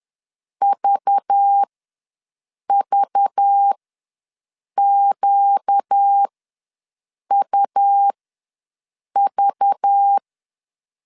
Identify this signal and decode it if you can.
Morse code, heard as a keyed tone, about 10 wpm, spaced as for 6 wpm, 800 Hz. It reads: VVQUV
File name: LEZIONI / 5 LBVC / 5 LBV ws 10wpm 3645